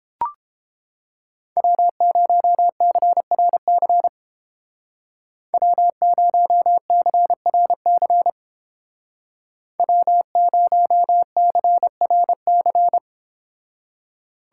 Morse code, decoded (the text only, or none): W0CRC W0CRC W0CRC